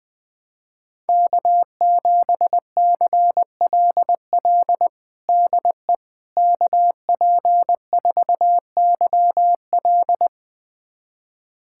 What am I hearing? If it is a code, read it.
Morse code, heard as a keyed tone, about 20 wpm, 700 Hz. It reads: K7CLL DE KP4YL
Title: USA Callsign DE Callsign - 5 Characters - 20wpm